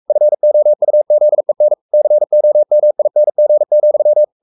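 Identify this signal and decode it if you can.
Morse code, heard as a keyed tone, about 44 wpm, 600 Hz. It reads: FOUZEN COMING,
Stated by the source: Morse code at 43 wpm sending Ken Follett's novel Never in English, from CW med Gnister.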